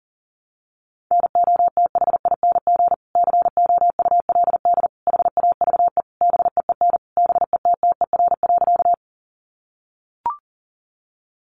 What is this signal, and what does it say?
Morse code, heard as a keyed tone, about 40 wpm, 700 Hz. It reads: NOTHING COULD HAVE BEEN BETTER.
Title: Sentences from Top 500 Words 40wpm